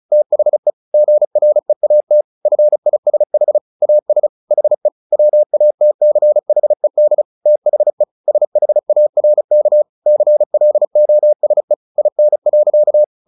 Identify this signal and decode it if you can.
Morse code, heard as a keyed tone, about 34 wpm, 600 Hz. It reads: THE GREAT FISH AS HE WATCHED THE SHARK CLOSE IN.
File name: the old man and the sea 2 35wpm